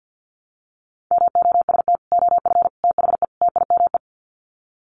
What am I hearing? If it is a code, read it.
Morse code, heard as a keyed tone, about 48 wpm, 700 Hz. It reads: MOST OF THE TIME
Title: Sets of 4 Words 50wpm